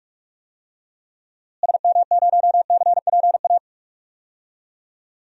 Morse code, heard as a keyed tone, about 43 wpm, 700 Hz. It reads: SM0CPA
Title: DXCC-100 Call Signs - 6 Characters - Repeated 3x - 45wpm